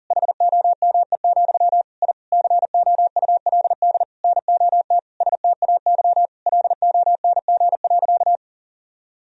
Morse code, heard as a keyed tone, about 40 wpm, 700 Hz. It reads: HOME, I COULD NOT STAY LONG.
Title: Sentences from Top 1000 Words - Review Part 3 - 40wpm